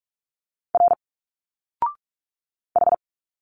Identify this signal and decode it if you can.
Morse code, heard as a keyed tone, about 42 wpm, 700 Hz. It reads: R H
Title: Single Letters 45wpm